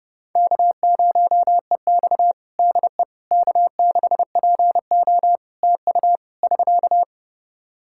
Morse code, heard as a keyed tone, about 32 wpm, 700 Hz. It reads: K0EX DE K6PO TU <SK>
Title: Practice QSO - Exchange 4 - Ending - 30wpm